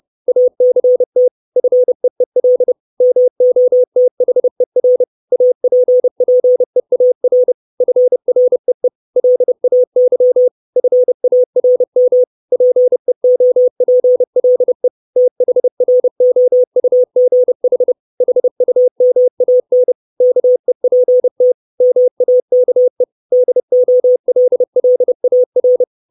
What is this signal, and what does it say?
Morse code, heard as a keyed tone, about 30 wpm, 500 Hz. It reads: ACT FEEL MOTHER APPEAR FREE LAY FARM PEOPLE THROUGH HUMAN KEPT MAKE DOLLAR
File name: 632 Words 30WPM